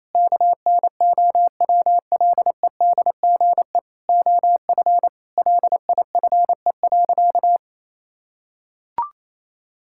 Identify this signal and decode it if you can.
Morse code, heard as a keyed tone, about 28 wpm, 700 Hz. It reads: KNOWLEDGE OF LIFE.
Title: Sentences from Top 1000 Words - Review Part 6 - 28wpm